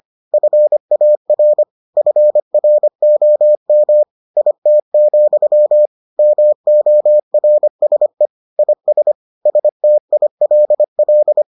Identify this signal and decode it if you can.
Morse code, heard as a keyed tone, about 25 wpm, 600 Hz. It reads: FAR FROM IT, MORSE IS STILL